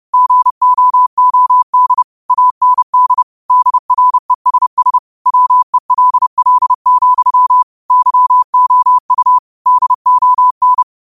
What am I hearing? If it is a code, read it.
Morse code, heard as a keyed tone, about 30 wpm, 1000 Hz. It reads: GOOD AND DRESS WELL, YOU DON
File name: qotdfiles / 30wpm